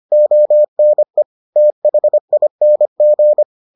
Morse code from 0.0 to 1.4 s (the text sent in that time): ONE